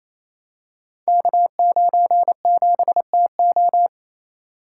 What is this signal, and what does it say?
Morse code, heard as a keyed tone, about 28 wpm, 700 Hz. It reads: K97TO
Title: Code Groups - 5 Letter:Numbers - 28wpm